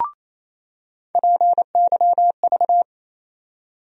Morse code, heard as a keyed tone, about 28 wpm, 700 Hz. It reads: PYV
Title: USA Call Sign Postfixes - 3 Letters 28wpm